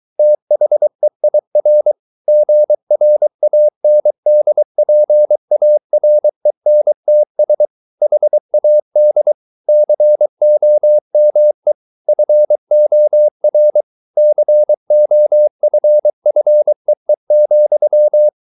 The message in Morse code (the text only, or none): THEIR GRANDPARENTS HAD COME FOR COFFEE,